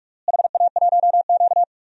Morse code, heard as a keyed tone, about 46 wpm, 700 Hz. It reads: HA1Q
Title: DXCC-100 Call Signs - 4 Characters - Repeated 3x - 45wpm